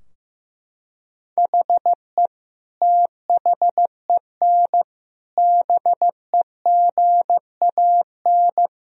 Morse code, heard as a keyed tone, about 15 wpm, 700 Hz. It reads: HE THEN BEGAN
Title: Sentences from Top 1000 Words - Review Part 3 - 15wpm